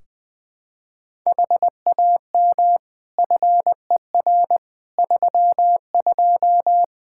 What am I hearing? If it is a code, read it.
Morse code, heard as a keyed tone, about 21 wpm, 700 Hz. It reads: HAM FER 32
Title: K1USN SST QSO Practice 20wpm